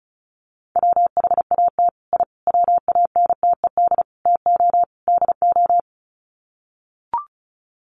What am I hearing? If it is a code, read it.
Morse code, heard as a keyed tone, about 34 wpm, 700 Hz. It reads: WHAT I WANTED TO DO
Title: Sets of 5 Words Encore2 35wpm